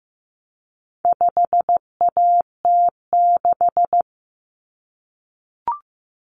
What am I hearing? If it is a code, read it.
Morse code, heard as a keyed tone, about 15 wpm, 700 Hz. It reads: 5AT6
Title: Common Cut Numbers - 4-Digit 15wpm